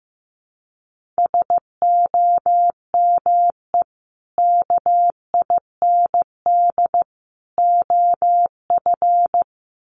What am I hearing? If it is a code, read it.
Morse code, heard as a keyed tone, about 15 wpm, 700 Hz. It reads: SOME KIND OF
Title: Sets of 3 Words 15wpm